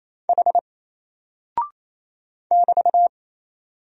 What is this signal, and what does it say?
Morse code, heard as a keyed tone, about 28 wpm, 700 Hz. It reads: H <BT>
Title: Single Character (Letter:Number:Punctuation:Prosign) Alt 28wpm